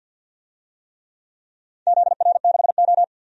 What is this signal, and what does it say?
Morse code, heard as a keyed tone, about 47 wpm, 700 Hz. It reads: ZR6O
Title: DXCC-100 Call Signs - 4 Characters - 50wpm